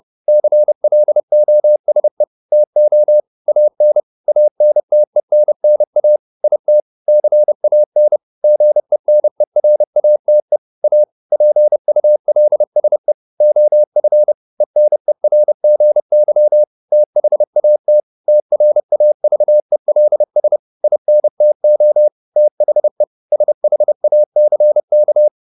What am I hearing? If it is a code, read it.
Morse code, heard as a keyed tone, about 30 wpm, 600 Hz. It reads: CLOSE TO AN ANTENNA IT CAN GENERATE A PULSE OF ENERGY THAT TRAVELS INTO THE SHACK